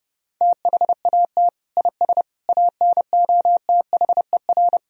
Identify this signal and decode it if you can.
Morse code, heard as a keyed tone, about 30 wpm, 700 Hz. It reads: THAT IS ANOTHER
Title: Sentences from Top 1000 Words - Review Part 1 - 30wpm